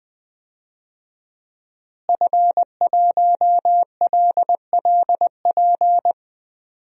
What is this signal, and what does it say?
Morse code, heard as a keyed tone, about 20 wpm, 700 Hz. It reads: F1LLP
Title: DXCC-100 Call Signs - 5 Characters - Repeated 2x - 20wpm